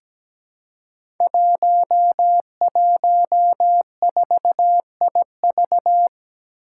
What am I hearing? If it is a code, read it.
Morse code, heard as a keyed tone, about 17 wpm, 700 Hz. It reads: 114IV
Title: Five Challenging Letter-Number Combinations 17wpm